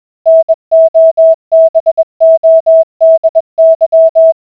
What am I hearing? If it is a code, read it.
Morse code, heard as a keyed tone, about 21 wpm, 650 Hz. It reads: NOBODY